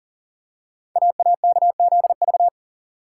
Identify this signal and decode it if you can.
Morse code, heard as a keyed tone, about 39 wpm, 700 Hz. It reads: AAKZV
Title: Code Groups - 5 Letter:Numbers - 40wpm